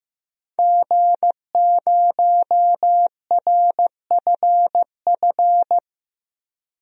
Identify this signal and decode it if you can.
Morse code, heard as a keyed tone, about 16 wpm, 700 Hz. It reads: G0RFF
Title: DXCC-100 Call Signs - 5 Characters - 15wpm